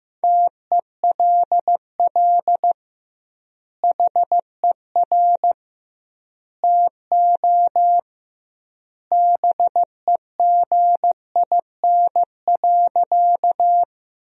Morse code, heard as a keyed tone, about 15 wpm, 700 Hz. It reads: TELL HER TO BEGIN.
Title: 2x Word-Spacing - Sentences from Top 300 Words - 15wpm